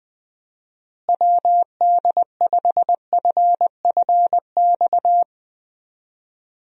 Morse code, heard as a keyed tone, about 20 wpm, 700 Hz. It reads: WD5FFX
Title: USA Call Signs - 6 Characters - Repeated 3x - 20wpm